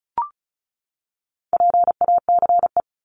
Morse code, heard as a keyed tone, about 34 wpm, 700 Hz. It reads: PACE
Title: Top 1751-2000 Words 35wpm